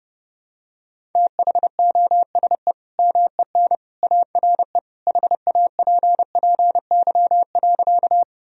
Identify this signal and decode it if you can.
Morse code, heard as a keyed tone, about 32 wpm, 700 Hz. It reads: THOSE MEN ARE HAPPY.
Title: Sentences from Top 1000 Words - Review Part 1 - 30wpm